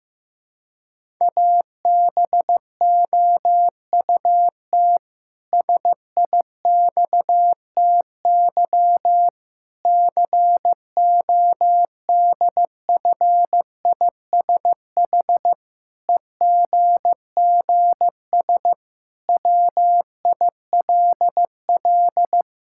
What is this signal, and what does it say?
Morse code, heard as a keyed tone, about 15 wpm, 700 Hz. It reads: ABOUT SIXTY CODFISH EGGS WILL
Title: Pangram Challenge v2 - 15wpm